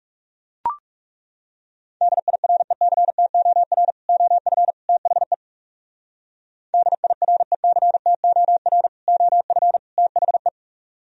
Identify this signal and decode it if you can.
Morse code, heard as a keyed tone, about 38 wpm, 700 Hz. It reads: DIRECTOR OF THE DIRECTOR OF THE